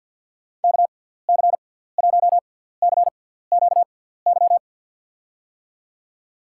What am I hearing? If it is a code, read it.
Morse code, heard as a keyed tone, about 48 wpm, spaced as for 17 wpm, 700 Hz. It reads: KC1CQY
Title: USA Call Signs - 6 Characters - 6x Character Spacing - 50wpm